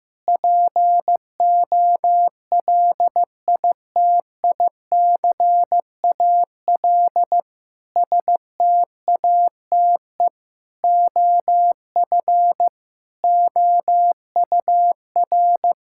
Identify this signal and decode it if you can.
Morse code, heard as a keyed tone, about 16 wpm, 700 Hz. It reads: POLITICAL STATE OF OUR
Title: Sentences from Top 1000 Words - Review Part 5 - 15wpm